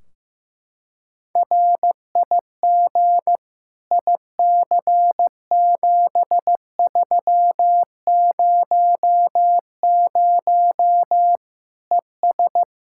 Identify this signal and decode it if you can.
Morse code, heard as a keyed tone, about 16 wpm, 700 Hz. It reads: RIG IC7300 ES
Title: Practice QSO - Exchange 2 - Info - 15wpm